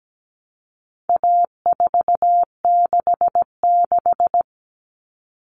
Morse code, heard as a keyed tone, about 17 wpm, 700 Hz. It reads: A466
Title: Common Cut Numbers - 4-Digit 17wpm